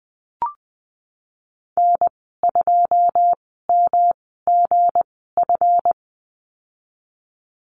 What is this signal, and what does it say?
Morse code, heard as a keyed tone, about 20 wpm, spaced as for 15 wpm, 700 Hz. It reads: N2MGF
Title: 161 Lesson 26g - Letter g - Call Signs 20wpm